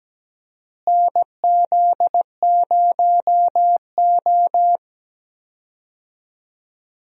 Morse code, heard as a keyed tone, about 17 wpm, 700 Hz. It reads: NZ0O
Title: USA Call Signs - 4 Characters - Repeated 2x - 17wpm